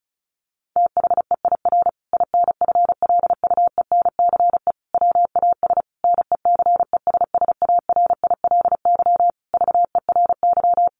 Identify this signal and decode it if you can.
Morse code, heard as a keyed tone, about 34 wpm, 700 Hz. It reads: THEIR INFLUENCE WAS NECESSARILY VERY
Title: Sentences from Top 1400 Words 35wpm